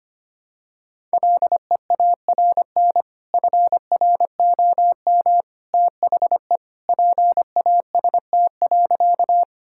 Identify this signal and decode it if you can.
Morse code, heard as a keyed tone, about 25 wpm, 700 Hz. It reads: LEARN FROM THE PAST.
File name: practice / 25wpm / Sentences with 2-Word Phrases - 25wpm